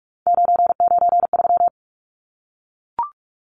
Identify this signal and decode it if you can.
Morse code, heard as a keyed tone, about 44 wpm, 700 Hz. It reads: 993
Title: Numbers - 3-Digit 45wpm